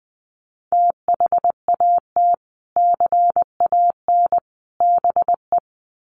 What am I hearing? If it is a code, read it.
Morse code, heard as a keyed tone, about 20 wpm, 700 Hz. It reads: THAT CAN BE
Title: Sets of 3 Words 20wpm